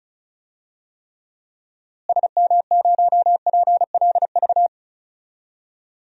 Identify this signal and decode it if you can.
Morse code, heard as a keyed tone, about 36 wpm, 700 Hz. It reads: SM0PLV
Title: DXCC-100 Call Signs - 6 Characters - Repeated 2x - 35wpm